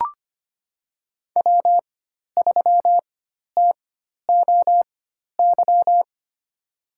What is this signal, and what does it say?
Morse code, heard as a keyed tone, about 25 wpm, spaced as for 12 wpm, 700 Hz. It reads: W3TOY
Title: USA Call Signs - 5 Characters - 4x Character Spacing - 25wpm